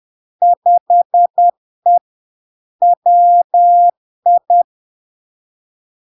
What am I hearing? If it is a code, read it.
Morse code, heard as a keyed tone, about 10 wpm, 700 Hz. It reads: 5E WI
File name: LARK-FD-2022-Log-in-CW-10wpm-700-Hz